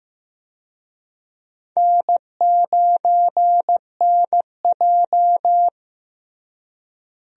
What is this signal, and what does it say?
Morse code, heard as a keyed tone, about 15 wpm, 700 Hz. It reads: N9NJ
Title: USA Call Signs - 4 Characters - Repeated 2x - 15wpm